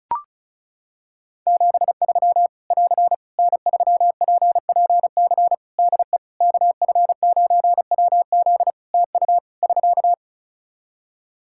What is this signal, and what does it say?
Morse code, heard as a keyed tone, about 36 wpm, 700 Hz. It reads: 73 <AR> N3PPC DE KF9WZ TU <SK>
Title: K1USN SST QSO Practice 35wpm